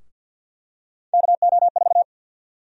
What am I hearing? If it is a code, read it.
Morse code, heard as a keyed tone, about 52 wpm, 700 Hz. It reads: KO4
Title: USA Call Sign Prefixes - Repeated 2x - 50wpm